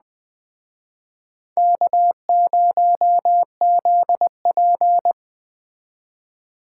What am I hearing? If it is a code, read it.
Morse code, heard as a keyed tone, about 19 wpm, 700 Hz. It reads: K0ZP